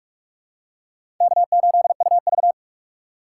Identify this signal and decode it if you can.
Morse code, heard as a keyed tone, about 46 wpm, 700 Hz. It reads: K8UV
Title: USA Call Signs - 2x Character Spacing - 45wpm